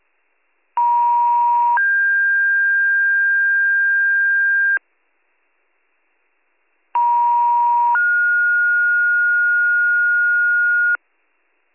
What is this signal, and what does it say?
Two-tone page, heard as a one-second tone, then a three-second tone, about 950 then 1600 Hz, about 950 then 1430 Hz, with noise. It is heard from 0.8 to 11.0 s.